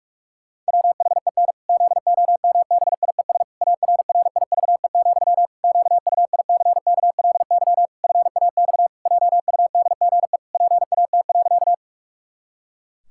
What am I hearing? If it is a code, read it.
Morse code, heard as a keyed tone, about 41 wpm, 700 Hz. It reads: WHEN ZOMBIES ARRIVE, QUICKLY FAX JUDGE PAT.
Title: Pangram Challenge v2 - 45wpm